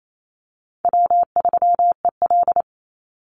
Morse code, heard as a keyed tone, about 28 wpm, 700 Hz. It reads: W3EL
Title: Easy Call Signs Repeated 2x - 28wpm